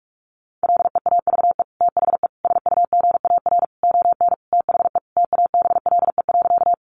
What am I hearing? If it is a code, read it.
Morse code, heard as a keyed tone, about 44 wpm, 700 Hz. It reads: LEAVE THE SUGAR ON THE TABLE.
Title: Sentences from Top 1200 Words 45wpm